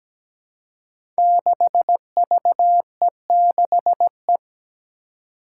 Morse code, heard as a keyed tone, about 17 wpm, 700 Hz. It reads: 6VE6E
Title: Five Challenging Letter-Number Combinations 17wpm